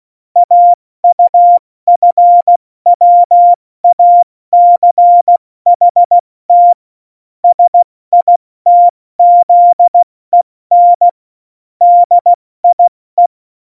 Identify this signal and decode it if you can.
Morse code, heard as a keyed tone, about 16 wpm, 700 Hz. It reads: AUFWACHT SITZEN DIE
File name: Morsekurs / Witze / Witz 09. Oktober 2018, 70 F80